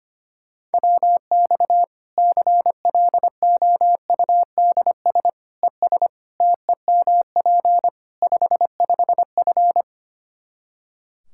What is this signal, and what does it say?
Morse code, heard as a keyed tone, about 25 wpm, 700 Hz. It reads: WX CLOUDS ES TEMP 55F
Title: K1USN SST QSO Practice 25wpm